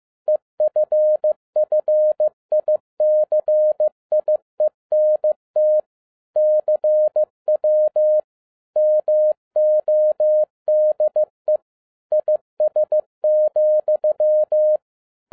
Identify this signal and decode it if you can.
Morse code, heard as a keyed tone, about 15 wpm, 610 Hz. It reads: EFFICIENT CW MODE IS,
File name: 2011-07-mp3 / n42 EXCITING QRP CW OPERATION 15wpm